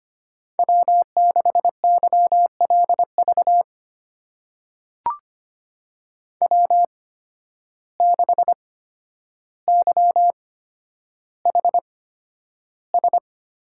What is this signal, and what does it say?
Morse code, heard as a keyed tone, about 25 wpm, spaced as for 11 wpm, 700 Hz. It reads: W6YLV W6YHS
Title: USA Call Signs - 5 Characters - 8x Character Spacing - 25wpm